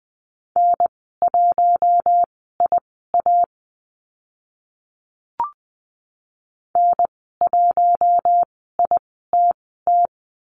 Morse code, heard as a keyed tone, about 20 wpm, spaced as for 14 wpm, 700 Hz. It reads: N1IA N1ITT
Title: 042 Lesson 8g - Number 1 - Call Signs 20wpm